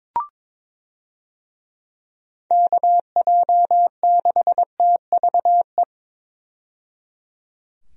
Morse code, heard as a keyed tone, about 22 wpm, 700 Hz. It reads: KJ6TVE
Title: USA Call Signs - 6 Characters - 22wpm